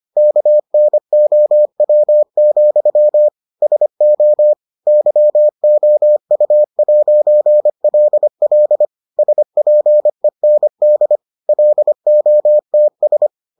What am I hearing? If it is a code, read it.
Morse code, heard as a keyed tone, about 25 wpm, 600 Hz. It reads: KNOW, SO YOU'LL SPEND LOTS